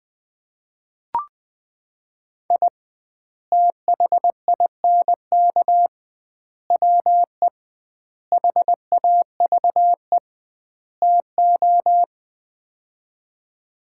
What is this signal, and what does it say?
Morse code, heard as a keyed tone, about 20 wpm, 700 Hz. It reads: I THINK WE HAVE TO
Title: 2x Word-Spacing - Sets of 5 Words - 20wpm